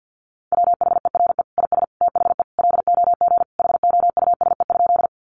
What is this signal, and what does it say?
Morse code, heard as a keyed tone, about 50 wpm, 700 Hz. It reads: WHERE IS THE LOG HOUSE?
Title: Sentences from Top 1400 Words 50wpm